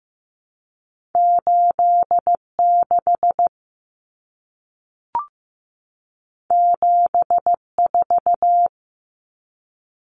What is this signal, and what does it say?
Morse code, heard as a keyed tone, about 15 wpm, 700 Hz. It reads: 8 6 7 4
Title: Common Cut Numbers - 2-Digit 15wpm